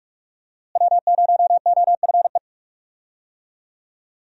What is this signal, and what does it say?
Morse code, heard as a keyed tone, about 46 wpm, 700 Hz. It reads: W0OFE
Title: USA Call Signs - 5 Characters - Repeated 3x - 45wpm